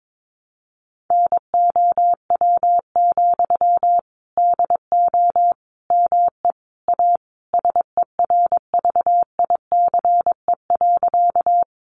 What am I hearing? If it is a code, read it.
Morse code, heard as a keyed tone, about 22 wpm, 700 Hz. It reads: NOW, DO ME A SERVICE.
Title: Sentences from Top 700 Words 22wpm